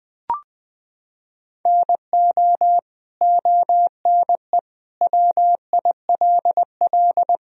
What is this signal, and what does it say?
Morse code, heard as a keyed tone, about 20 wpm, 700 Hz. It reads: NO ONE WILL
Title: Sentences from Top 100 Words and Abbreviations 20wpm